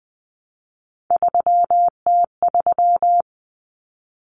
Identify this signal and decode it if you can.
Morse code, heard as a keyed tone, about 20 wpm, 700 Hz. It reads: 3T3
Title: Common Cut Numbers - 3-Digit 20wpm